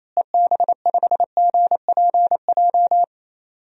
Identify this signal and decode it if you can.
Morse code, heard as a keyed tone, about 28 wpm, 700 Hz. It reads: EB5GPJ
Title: DXCC-100 Call Signs - 6 Characters - Repeated 2x - 28wpm